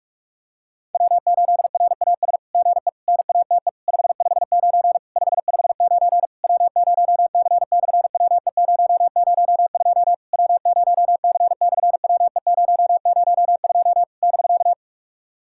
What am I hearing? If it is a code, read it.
Morse code, heard as a keyed tone, about 48 wpm, 700 Hz. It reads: W8RAS GE NATE 559 559 W0C/WE002 W0C/WE002 <BK>
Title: SOTA QSO Practice 45wpm